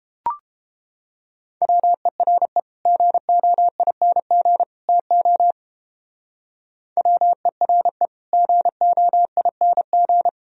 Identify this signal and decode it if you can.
Morse code, heard as a keyed tone, about 29 wpm, 700 Hz. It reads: WERE GOING TO WERE GOING